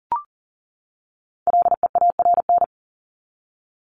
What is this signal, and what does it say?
Morse code, heard as a keyed tone, about 39 wpm, 700 Hz. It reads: LEARN
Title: ED Suffix - Top 100 Words with Base Words 40wpm